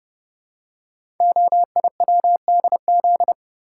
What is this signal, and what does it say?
Morse code, heard as a keyed tone, about 30 wpm, 700 Hz. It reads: OIWDZ
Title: Code Groups - 5 Letters - 30wpm